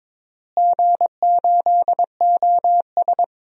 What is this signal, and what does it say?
Morse code, heard as a keyed tone, about 22 wpm, 700 Hz. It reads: G8OS
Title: DXCC-100 Call Signs - 4 Characters - Repeated 2x - 22wpm